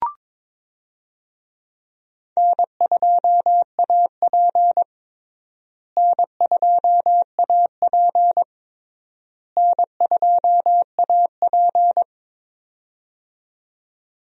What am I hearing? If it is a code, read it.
Morse code, heard as a keyed tone, about 22 wpm, 700 Hz. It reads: N2AP N2AP N2AP
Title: USA Call Signs - 4 Characters - Repeated 3x - 22wpm